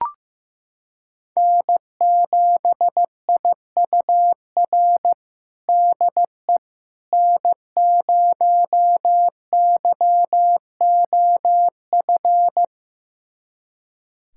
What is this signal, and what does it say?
Morse code, heard as a keyed tone, about 15 wpm, 700 Hz. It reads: N7IUR DE N0YOF
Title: USA Callsign DE Callsign - 5 Characters - 15wpm